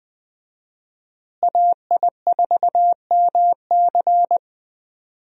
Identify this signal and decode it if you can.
Morse code, heard as a keyed tone, about 20 wpm, 700 Hz. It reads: AI4MC
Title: USA Call Signs - 5 Characters - Repeated 3x - 20wpm